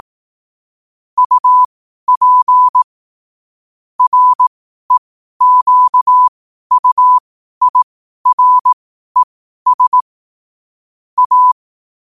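Morse code, heard as a keyed tone, about 18 wpm, spaced as for 12 wpm, 1000 Hz. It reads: UP REQUIRES A
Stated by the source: Morse code quotes at 12 words per minute.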